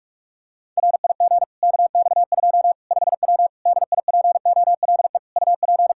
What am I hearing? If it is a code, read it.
Morse code, heard as a keyed tone, about 41 wpm, 700 Hz. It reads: RIG KX2 5W DIPOLE UP
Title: Practice QSO - Exchange 2 - Info - 45wpm